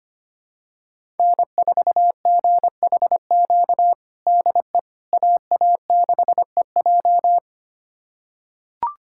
Morse code, heard as a keyed tone, about 25 wpm, 700 Hz. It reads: N4GHQ DE AA6EJ
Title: USA Callsign DE Callsign - 5 Characters - 25wpm